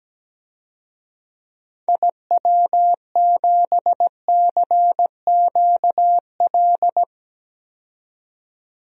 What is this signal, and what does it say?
Morse code, heard as a keyed tone, about 17 wpm, 700 Hz. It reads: IW7CQL
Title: DXCC-100 Call Signs - 6 Characters - 17wpm